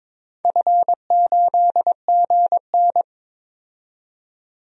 Morse code, heard as a keyed tone, about 22 wpm, 700 Hz. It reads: F8GN
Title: DXCC-100 Call Signs - 4 Characters - 22wpm